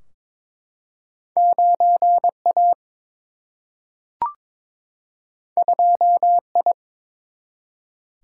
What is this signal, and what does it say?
Morse code, heard as a keyed tone, about 22 wpm, 700 Hz. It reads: 9A 2I
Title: Single-Letter Single-Number 22wpm